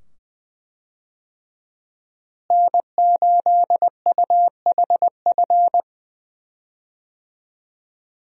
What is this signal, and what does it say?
Morse code, heard as a keyed tone, about 20 wpm, 700 Hz. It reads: N8UHF
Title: USA Call Signs - 5 Characters - Repeated 2x - 20wpm